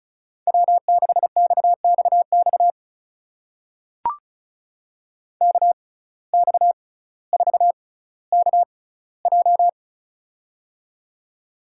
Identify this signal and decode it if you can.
Morse code, heard as a keyed tone, about 36 wpm, spaced as for 18 wpm, 700 Hz. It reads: W6XXX KX4KJ
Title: USA Call Signs - 5 Characters - 6x Character Spacing - 35wpm